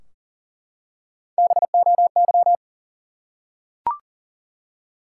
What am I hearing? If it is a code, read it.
Morse code, heard as a keyed tone, about 40 wpm, 700 Hz. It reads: BOY
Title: Three Letter Words - 40wpm